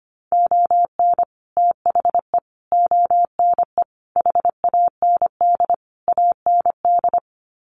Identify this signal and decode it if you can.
Morse code, heard as a keyed tone, about 25 wpm, 700 Hz. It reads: ON THE ONE HAND AND
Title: Sets of 5 Words Encore2 25wpm